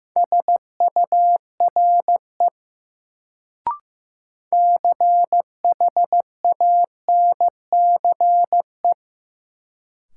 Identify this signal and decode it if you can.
Morse code, heard as a keyed tone, about 15 wpm, 700 Hz. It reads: SURE CHANCE
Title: Sets of 3 Words - New Word Drill - 15wpm